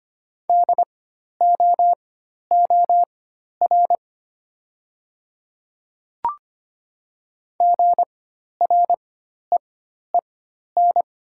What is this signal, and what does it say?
Morse code, heard as a keyed tone, about 25 wpm, spaced as for 12 wpm, 700 Hz. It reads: DOOR GREEN
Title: 4x Character Spacing - Top 500 Words - 25wpm